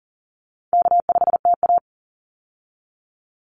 Morse code, heard as a keyed tone, about 39 wpm, 700 Hz. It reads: K5TA
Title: Easy Call Signs Repeated 2x - 40wpm